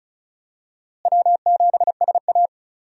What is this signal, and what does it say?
Morse code, heard as a keyed tone, about 34 wpm, 700 Hz. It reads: W7SA
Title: USA Call Signs - 4 Characters - Repeated 2x - 35wpm